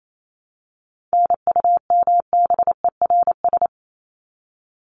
Morse code, heard as a keyed tone, about 28 wpm, 700 Hz. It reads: NUMBERS
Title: ES:S Suffix - Top 100 Words with Base Words 28wpm